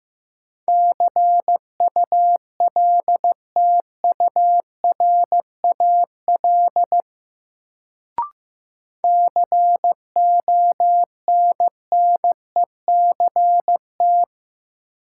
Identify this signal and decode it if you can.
Morse code, heard as a keyed tone, about 15 wpm, 700 Hz. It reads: CULTURAL CONNECT
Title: Sentences from Top 1500 Words - New Word Drill - 15wpm